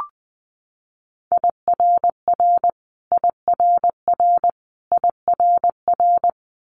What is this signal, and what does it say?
Morse code, heard as a keyed tone, about 20 wpm, 700 Hz. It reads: IRR IRR IRR
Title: Multi-letter Phonemes Repeated 3x 20wpm